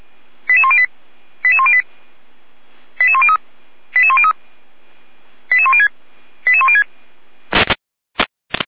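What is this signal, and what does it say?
Five-tone selective call (ZVEI), heard as quick runs of five tones, tones of about 70 ms. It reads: 88108 88108 88103 88103 88106 88106
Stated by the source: ZVEI five-tone call received on SDR.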